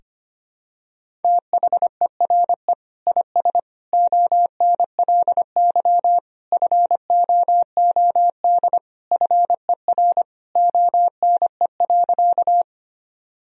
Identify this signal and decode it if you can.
Morse code, heard as a keyed tone, about 25 wpm, 700 Hz. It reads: THERE IS ONLY FOOD FER ONE.
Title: Sentences from Top 300 Words and Abbreviations 25wpm